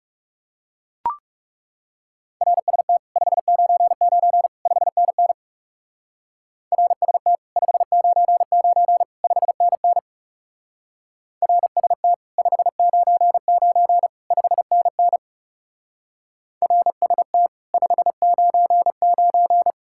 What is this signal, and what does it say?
Morse code, heard as a keyed tone, about 36 wpm, 700 Hz. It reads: RST 599 5NN RST 599 5NN RST 599 5NN RST 599
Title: Speed-Racing Common QSO Phrases - 30wpm